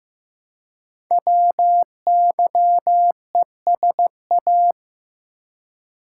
Morse code, heard as a keyed tone, about 15 wpm, 700 Hz. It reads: WYESA
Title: Code Groups - 5 Letters - 15wpm